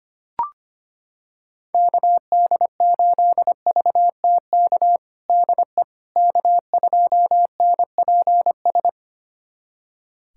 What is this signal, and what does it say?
Morse code, heard as a keyed tone, about 25 wpm, 700 Hz. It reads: KD8VTK DE K2NPS
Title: USA Callsign DE Callsign 25wpm